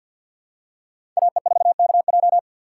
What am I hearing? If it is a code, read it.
Morse code, heard as a keyed tone, about 52 wpm, 690 Hz. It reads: AE4KJ
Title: USA Call Signs - 4x Character Spacing - 50wpm